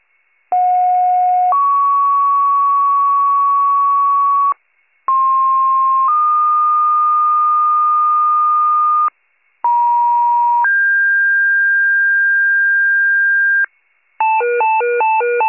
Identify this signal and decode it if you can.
Two-tone page, heard as a one-second tone, then a three-second tone, about 730 then 1120 Hz, about 1060 then 1220 Hz, about 950 then 1600 Hz. It is heard from 0.5 to 13.7 s.